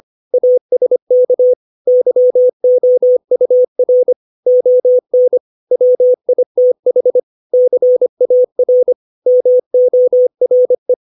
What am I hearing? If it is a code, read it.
Morse code, heard as a keyed tone, about 25 wpm, 500 Hz. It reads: ASK YOUR ON WITH CAR MORE
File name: 632 words / 632 Words 25WPM